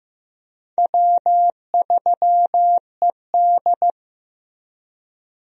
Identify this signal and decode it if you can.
Morse code, heard as a keyed tone, about 15 wpm, 700 Hz. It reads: W3ED
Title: USA Call Signs - 4 Characters - Repeated 2x - 15wpm